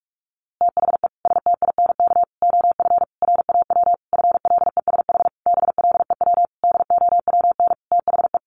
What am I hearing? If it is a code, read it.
Morse code, heard as a keyed tone, about 46 wpm, 700 Hz. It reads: THE STINK OF RAW FLESH BLEW DOWN THE
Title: A Sound Of Thunder 45wpm